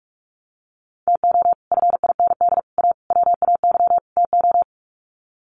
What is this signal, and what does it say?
Morse code, heard as a keyed tone, about 44 wpm, 700 Hz. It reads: TO FIND A WAY TO
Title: Sets of 5 Words 45wpm